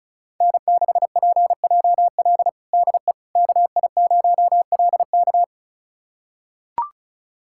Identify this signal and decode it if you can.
Morse code, heard as a keyed tone, about 36 wpm, 700 Hz. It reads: N6PJL DE KI0LK
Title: USA Callsign DE Callsign - 5 Characters - 35wpm